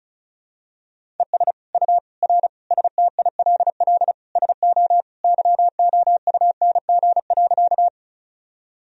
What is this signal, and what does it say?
Morse code, heard as a keyed tone, about 36 wpm, 700 Hz. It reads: ES U R STILL SO YOUNG.
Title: Sentences from Top 400 Words and Abbreviations 35wpm